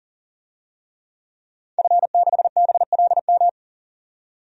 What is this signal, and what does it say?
Morse code, heard as a keyed tone, about 40 wpm, 700 Hz. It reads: F6BLM
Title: DXCC-100 Call Signs - 5 Characters - 40wpm